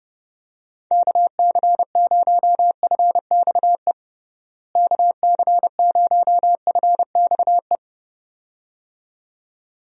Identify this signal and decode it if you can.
Morse code, heard as a keyed tone, about 30 wpm, 700 Hz. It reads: KC0FXE KC0FXE
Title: USA Call Signs - 6 Characters - Repeated 2x - 30wpm